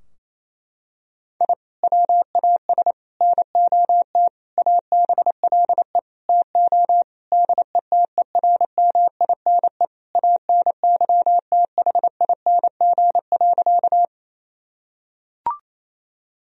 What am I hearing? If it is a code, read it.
Morse code, heard as a keyed tone, about 27 wpm, 700 Hz. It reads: I WAS NOT ABLE TO DETERMINE ANYTHING.
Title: Sentences from Top 1000 Words - Review Part 4 - 28wpm